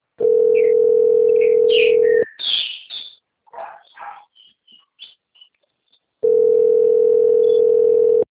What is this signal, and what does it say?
Ringback tone (North America), 440 + 480 Hz.
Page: Dourados - MS